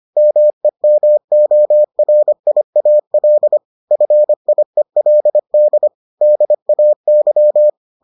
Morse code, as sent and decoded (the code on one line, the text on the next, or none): -- . -- --- .-. .. .- .-.. / ..-. .. .-.. -.. / -.. .- -.--
MEMORIAL FIELD DAY